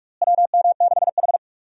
Morse code, heard as a keyed tone, about 42 wpm, 700 Hz. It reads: WM6H
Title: USA Call Signs - 4 Characters - Repeated 2x - 45wpm